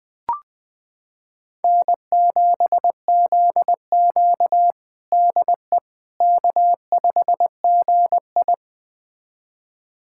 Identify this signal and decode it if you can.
Morse code, heard as a keyed tone, about 20 wpm, 700 Hz. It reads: N7ZQ DE K5GI
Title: USA Callsign DE Callsign - 4 Characters - 20wpm